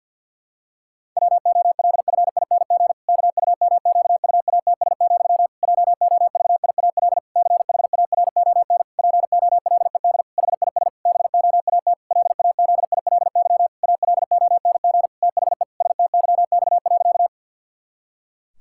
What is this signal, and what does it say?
Morse code, heard as a keyed tone, about 52 wpm, 700 Hz. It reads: WOLFING KUMQUATS, JOVIAL CHARON POLED HIS BOAT LAZILY ALONG THE STYX.
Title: Pangram Challenge v2 - 50wpm